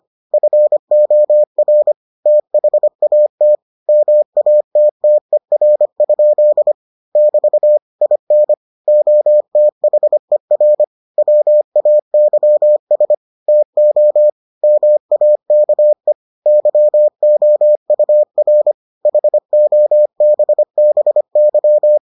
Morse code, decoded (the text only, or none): FOR THAT MATTER? <BT> IN OTHER WAYS TO MAKE YOUR HOBBY